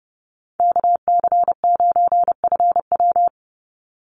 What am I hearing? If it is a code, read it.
Morse code, heard as a keyed tone, about 30 wpm, 700 Hz. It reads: KC9FW
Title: USA Call Signs 30wpm